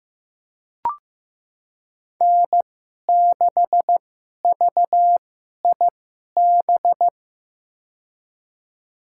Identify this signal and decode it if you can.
Morse code, heard as a keyed tone, about 15 wpm, spaced as for 11 wpm, 700 Hz. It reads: N6VIB
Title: USA Call Signs - 5 Characters - 2x Character Spacing - 15wpm